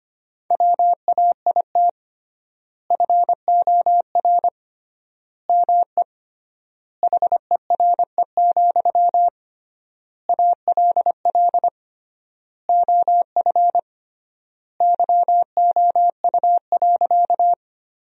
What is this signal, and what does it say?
Morse code, heard as a keyed tone, about 25 wpm, 700 Hz. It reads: WAIT FOR ME HERE, ALL OF YOU.
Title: 3x Word-Spacing - Sentences from Top 500 Words - 25wpm